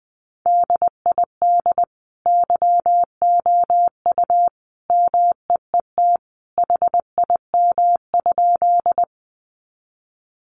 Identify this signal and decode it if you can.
Morse code, heard as a keyed tone, about 20 wpm, 700 Hz. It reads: DID YOU MEET HIM?
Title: Sentences from Top 1100 Words 20wpm